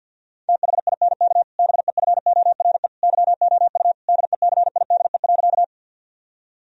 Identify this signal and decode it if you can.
Morse code, heard as a keyed tone, about 52 wpm, 700 Hz. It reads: THINK BEFORE YOU DECIDE.
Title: Sentences from Top 1000 Words - Review Part 3 - 50wpm